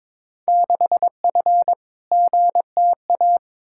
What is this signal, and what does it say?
Morse code, heard as a keyed tone, about 22 wpm, 700 Hz. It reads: 6F GTA
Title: Field Day Exchanges Encore - 22wpm